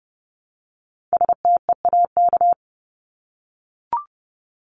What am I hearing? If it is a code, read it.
Morse code, heard as a keyed tone, about 30 wpm, 700 Hz. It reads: STEAK
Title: Top 1251-1500 Words 30wpm